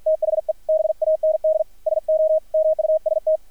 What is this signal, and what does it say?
Morse code, heard as a keyed tone, about 53 wpm, 630 Hz, with radio noise. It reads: THE BAND SO MUST
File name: archived / CRSnet